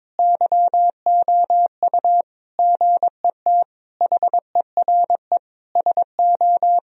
Morse code, heard as a keyed tone, about 22 wpm, 700 Hz. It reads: YOU GET HERE SO
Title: Sentences from Top 1000 Words - Review Part 2 - 22wpm